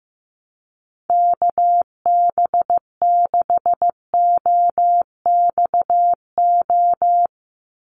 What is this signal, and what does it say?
Morse code, heard as a keyed tone, about 15 wpm, 700 Hz. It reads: KB6OXO
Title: USA Call Signs 15wpm